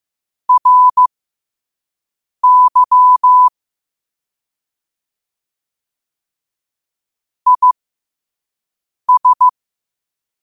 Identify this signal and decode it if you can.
Morse code, heard as a keyed tone, about 15 wpm, spaced as for 5 wpm, 1000 Hz. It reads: RY IS
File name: qotdfiles / 5wpm